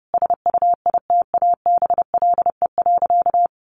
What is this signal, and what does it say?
Morse code, heard as a keyed tone, about 30 wpm, 700 Hz. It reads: SUITABLE.
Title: Sentences from Top 1400 Words 30wpm